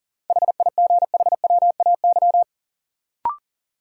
Morse code, heard as a keyed tone, about 40 wpm, 700 Hz. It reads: HIGHWAY
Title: Sentences from Top 1400 Words - New Word Drill - 40wpm